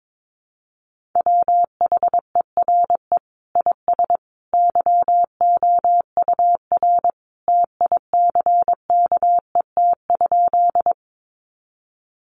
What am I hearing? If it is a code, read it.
Morse code, heard as a keyed tone, about 22 wpm, 700 Hz. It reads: WHERE IS YOUR TICKET?
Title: Sentences from Top 1400 Words 22wpm